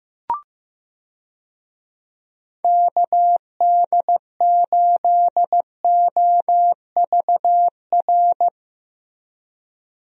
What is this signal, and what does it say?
Morse code, heard as a keyed tone, about 15 wpm, 700 Hz. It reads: KD8OVR
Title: USA Call Signs - 6 Characters - Repeated 2x - 15wpm